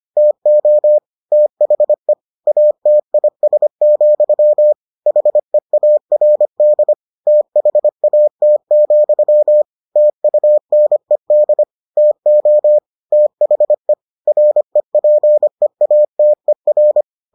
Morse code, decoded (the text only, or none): TO THE ATIS, HEARD THAT, TUNED TO THE REPEATER